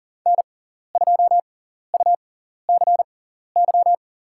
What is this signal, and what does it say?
Morse code, heard as a keyed tone, about 40 wpm, spaced as for 14 wpm, 700 Hz. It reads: N2UCY
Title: USA Call Signs - 5 Characters - 6x Character Spacing - 40wpm